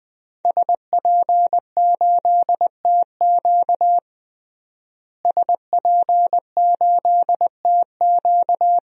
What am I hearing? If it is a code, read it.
Morse code, heard as a keyed tone, about 20 wpm, 700 Hz. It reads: SP8TQ SP8TQ
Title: DXCC-100 Call Signs - 5 Characters - Repeated 2x - 20wpm